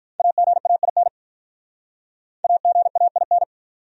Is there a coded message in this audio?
Yes: Morse code from 0.2 to 3.4 s, heard as a keyed tone, about 47 wpm, 700 Hz. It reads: AGAIN AGAIN